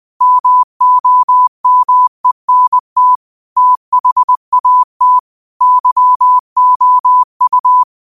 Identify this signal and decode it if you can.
Morse code, heard as a keyed tone, about 20 wpm, 1000 Hz. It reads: MOMENT THAT YOU
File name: qotdfiles / 20wpm